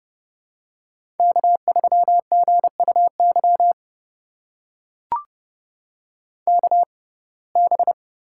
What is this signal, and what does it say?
Morse code, heard as a keyed tone, about 32 wpm, spaced as for 22 wpm, 700 Hz. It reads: K3GUY KB
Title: USA Call Signs - 6x Character Spacing - 30wpm